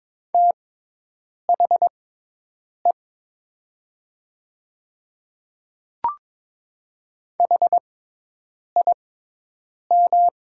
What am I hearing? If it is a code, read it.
Morse code, heard as a keyed tone, about 22 wpm, spaced as for 8 wpm, 700 Hz. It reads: THE HIM
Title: 6x Character Spacing - Top 100 Words - 22wpm